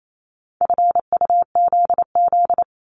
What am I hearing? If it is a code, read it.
Morse code, heard as a keyed tone, about 28 wpm, 700 Hz. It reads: FUZZ
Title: Multi-letter Phonemes 28wpm